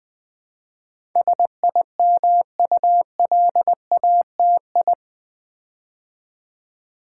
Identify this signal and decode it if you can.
Morse code, heard as a keyed tone, about 20 wpm, 700 Hz. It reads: SIMULATI
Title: Build A Word - 4 Letters At A Time - 10-Letter Words - 20wpm